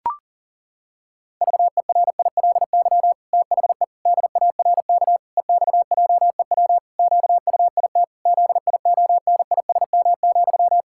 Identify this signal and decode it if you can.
Morse code, heard as a keyed tone, about 40 wpm, 700 Hz. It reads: VERILY THE DARK EXJEW QUIT ZIONISM,
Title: Pangram Challenge v2 - 40wpm